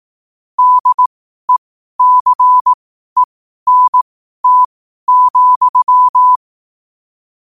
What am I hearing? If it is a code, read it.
Morse code, heard as a keyed tone, about 18 wpm, spaced as for 12 wpm, 1000 Hz. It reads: DECENT,
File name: qotdfiles / 12wpm